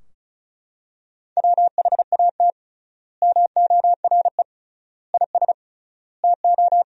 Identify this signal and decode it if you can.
Morse code, heard as a keyed tone, about 34 wpm, 700 Hz. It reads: WHAT MORE IS TO
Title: 3x Word-Spacing - Sentences from Top 500 Words - 35wpm